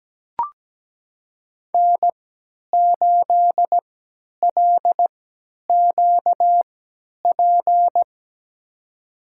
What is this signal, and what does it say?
Morse code, heard as a keyed tone, about 17 wpm, spaced as for 10 wpm, 700 Hz. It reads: N8LQP
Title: USA Call Signs - 5 Characters - 3x Character Spacing - 17wpm